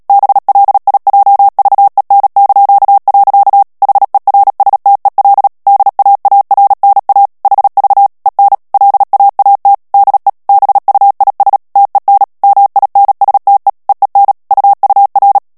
Morse code, heard as a keyed tone, about 37 wpm, 800 Hz. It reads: BLIJVEN). HERSTEL DAARNA HV EN LAAT DE BUIS TEN MINSTE EEN UUR
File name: morse mp3 / HOLLANDAIS texte clair 2 vitesse 35 mots minute